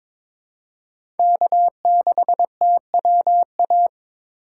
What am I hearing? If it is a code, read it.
Morse code, heard as a keyed tone, about 21 wpm, 700 Hz. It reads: K6TWA
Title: USA Call Signs - 5 Characters - 2x Character Spacing - 22wpm